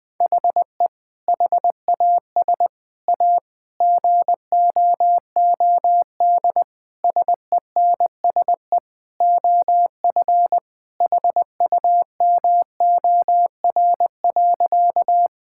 Morse code, heard as a keyed tone, about 20 wpm, 700 Hz. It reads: HE HAS A GOOD SENSE OF HUMOR.
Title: Sentences with 2-Word Phrases - 20wpm